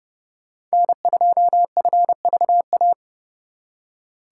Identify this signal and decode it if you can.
Morse code, heard as a keyed tone, about 36 wpm, 700 Hz. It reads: N2FVA
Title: Speed Racing - USA 1x3 Call Signs 30wpm